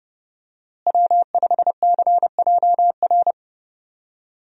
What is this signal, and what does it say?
Morse code, heard as a keyed tone, about 30 wpm, 700 Hz. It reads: W5CJR
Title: USA Call Signs - 5 Characters - Repeated 3x - 30wpm